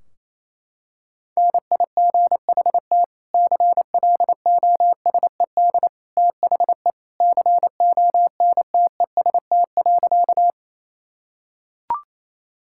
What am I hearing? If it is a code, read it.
Morse code, heard as a keyed tone, about 28 wpm, 700 Hz. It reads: NIGHT CLOSED THE CONTEST.
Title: Sentences from Top 1600 Words 28wpm